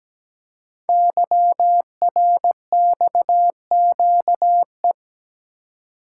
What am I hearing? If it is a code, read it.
Morse code, heard as a keyed tone, about 17 wpm, 700 Hz. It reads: YRXQE
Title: Code Groups - 5 Letters - 17wpm